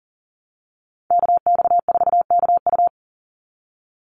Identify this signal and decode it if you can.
Morse code, heard as a keyed tone, about 39 wpm, 700 Hz. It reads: KX4KU
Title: Call Signs Repeated 2x - 40wpm